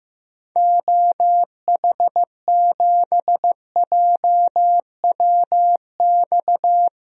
Morse code, heard as a keyed tone, about 15 wpm, 700 Hz. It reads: OH7JWX
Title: DXCC-100 Call Signs - 6 Characters - 15wpm